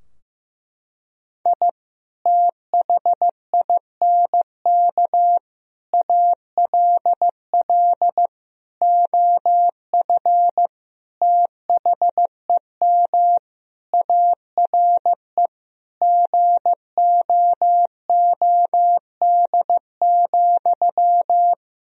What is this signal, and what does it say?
Morse code, heard as a keyed tone, about 15 wpm, 700 Hz. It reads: I THINK ALL OF THEM ARE GOOD,
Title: Sentences from Top 1000 Words - Review Part 4 - 15wpm